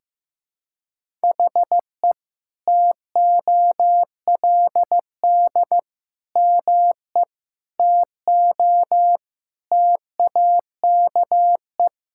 Morse code, heard as a keyed tone, about 15 wpm, 700 Hz. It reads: HE TOLD ME TO TAKE
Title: Sentences from Top 1000 Words - Review Part 6 - 15wpm